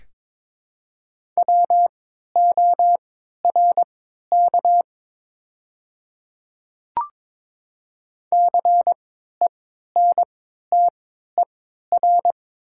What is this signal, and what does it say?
Morse code, heard as a keyed tone, about 22 wpm, spaced as for 13 wpm, 700 Hz. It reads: WORK CENTER
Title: 3x Character Spacing - Top 500 Words - 22wpm